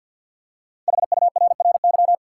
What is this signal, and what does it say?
Morse code, heard as a keyed tone, about 51 wpm, 700 Hz. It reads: HURRY
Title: Five Letter Words - 50wpm